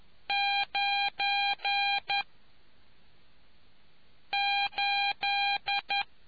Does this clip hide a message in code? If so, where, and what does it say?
Yes: Morse code from 0.3 to 6.0 s, heard as a keyed tone, about 11 wpm, 800 Hz. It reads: 9 8